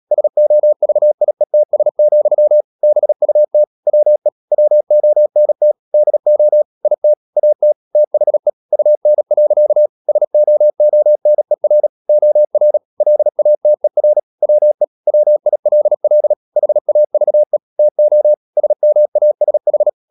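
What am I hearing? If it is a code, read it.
Morse code, heard as a keyed tone, about 37 wpm, 600 Hz. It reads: SOVIETS, BUT WE WONT DO IT AT THE UN. SOONER OR LATER WE WILL HAVE TO SMASH
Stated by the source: Morse code practice at 35–39 wpm sending English text of Ken Follett's novel Never, from CW med Gnister.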